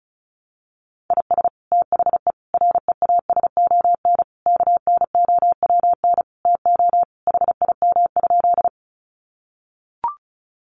Morse code, heard as a keyed tone, about 34 wpm, 700 Hz. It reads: IS THE REASON KNOWN TO HIM?
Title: Sentences from Top 700 Words 35wpm